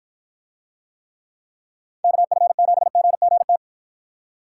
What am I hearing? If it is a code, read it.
Morse code, heard as a keyed tone, about 51 wpm, 700 Hz. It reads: KF7GGT